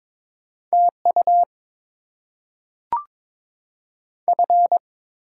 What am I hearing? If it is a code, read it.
Morse code, heard as a keyed tone, about 22 wpm, 700 Hz. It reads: TU F
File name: Top 20 QSO Elements - 6x Character Spacing - 22wpm